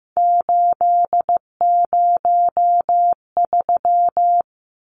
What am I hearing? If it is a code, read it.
Morse code, heard as a keyed tone, about 15 wpm, 700 Hz. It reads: 803
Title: Numbers - 3-Digit 15wpm